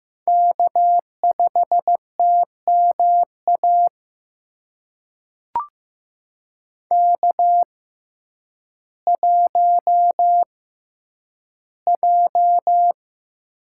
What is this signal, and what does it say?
Morse code, heard as a keyed tone, about 15 wpm, spaced as for 9 wpm, 700 Hz. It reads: K5TMA K1J
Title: USA Call Signs - 5 Characters - 6x Character Spacing - 15wpm